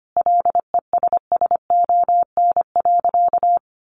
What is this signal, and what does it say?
Morse code, heard as a keyed tone, about 25 wpm, 700 Hz. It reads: LESSON.
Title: Sentences from Top 1400 Words 25wpm